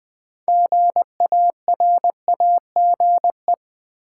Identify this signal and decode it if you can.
Morse code, heard as a keyed tone, about 20 wpm, 700 Hz. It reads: GARAGE
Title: Build A Word - 3 Letters At A Time - 6-Letter Words - 20wpm